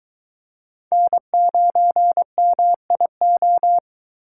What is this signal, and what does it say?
Morse code, heard as a keyed tone, about 24 wpm, 700 Hz. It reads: N9MIO